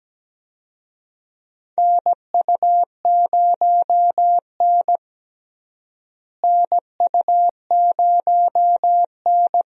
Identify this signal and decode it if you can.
Morse code, heard as a keyed tone, about 17 wpm, 700 Hz. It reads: NU0N NU0N
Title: USA Call Signs - 4 Characters - Repeated 2x - 17wpm